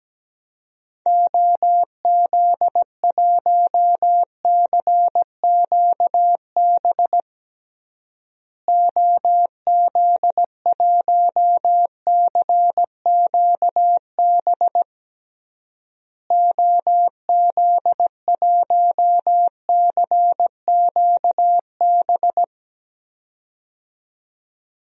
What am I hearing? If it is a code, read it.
Morse code, heard as a keyed tone, about 17 wpm, 700 Hz. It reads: OZ1CQB OZ1CQB OZ1CQB
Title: DXCC-100 Call Signs - 6 Characters - Repeated 3x - 17wpm